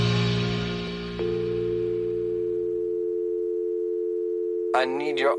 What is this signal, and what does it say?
Dial tone (North America), 350 + 440 Hz, with speech.
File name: wrct-log hi- 21